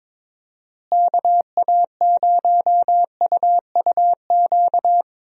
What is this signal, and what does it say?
Morse code, heard as a keyed tone, about 22 wpm, 700 Hz. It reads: KA0UUQ
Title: USA Call Signs - 6 Characters - Repeated 3x - 22wpm